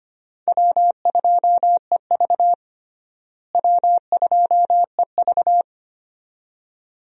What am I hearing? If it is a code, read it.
Morse code, heard as a keyed tone, about 25 wpm, 700 Hz. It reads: W2EV W2EV
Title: USA Call Signs - 4 Characters - Repeated 2x - 25wpm